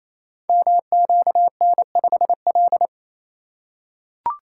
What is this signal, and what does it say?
Morse code, heard as a keyed tone, about 29 wpm, 700 Hz. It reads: MQN5L
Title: Code Groups - 5 Letter:Numbers - 28wpm